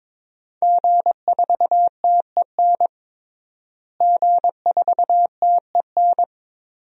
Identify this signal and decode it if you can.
Morse code, heard as a keyed tone, about 22 wpm, 700 Hz. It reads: G4TEN G4TEN
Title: DXCC-100 Call Signs - 5 Characters - Repeated 2x - 22wpm